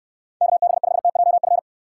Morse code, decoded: KB5TJF

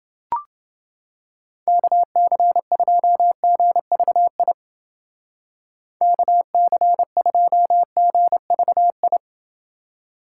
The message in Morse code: KC2GVI KC2GVI

